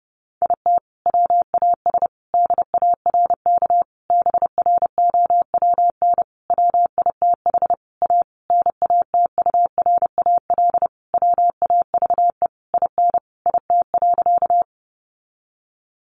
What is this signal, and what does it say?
Morse code, heard as a keyed tone, about 30 wpm, 700 Hz. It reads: IT WAS DARK BROWN WITH A NATURAL WAVE IN IT.
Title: Sentences from Top 1400 Words 30wpm